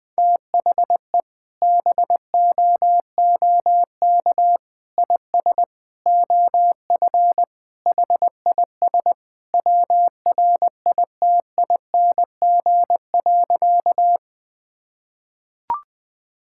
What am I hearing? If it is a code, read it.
Morse code, heard as a keyed tone, about 20 wpm, 700 Hz. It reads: THE BOOK IS OF HIS WRITING.
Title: Sentences with 2-Word Phrases - Spoken Beginning - 20wpm